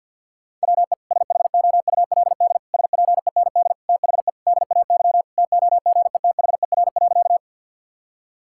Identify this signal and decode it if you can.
Morse code, heard as a keyed tone, about 51 wpm, 700 Hz. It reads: WE SHOULD SPEND THE DAY TOGETHER.
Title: Sentences from Top 1000 Words - Review Part 5 - 50wpm